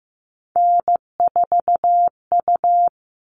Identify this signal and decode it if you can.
Morse code, heard as a keyed tone, about 15 wpm, 700 Hz. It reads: N4U
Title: All Cut Numbers - 3-Digit 15wpm